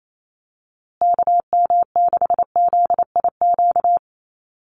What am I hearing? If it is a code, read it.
Morse code, heard as a keyed tone, about 28 wpm, 700 Hz. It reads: KM6ZIQ
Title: USA Call Signs Repeated 2x 28wpm